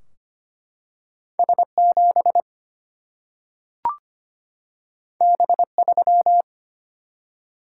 Morse code, heard as a keyed tone, about 25 wpm, 700 Hz. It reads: S7 B3
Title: Single-Letter Single-Number 25wpm